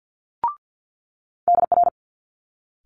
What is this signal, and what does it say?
Morse code, heard as a keyed tone, about 50 wpm, 700 Hz. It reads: DR
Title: Multi-letter Phonemes Quick Fire 50wpm